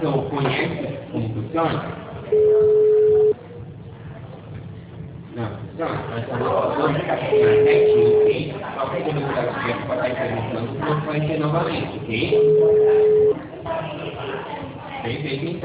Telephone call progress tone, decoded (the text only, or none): ringback tone